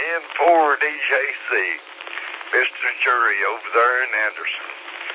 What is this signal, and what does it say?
Emergency Alert System (SAME) tones, heard as harsh buzzing bursts, with speech. It is heard from 0.9 to 4.4 s.